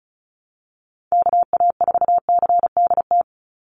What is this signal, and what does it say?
Morse code, heard as a keyed tone, about 34 wpm, 700 Hz. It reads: KA4CDT
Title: USA Call Signs 35wpm